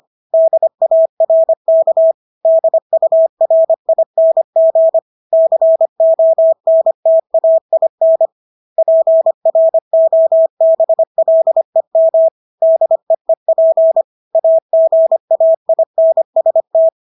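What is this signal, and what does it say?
Morse code, heard as a keyed tone, about 25 wpm, 650 Hz. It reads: DARK DURING CONTAIN PROBLEM DEEP AGAINST
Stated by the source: Morse practice of English words at 25 wpm from CW med Gnister.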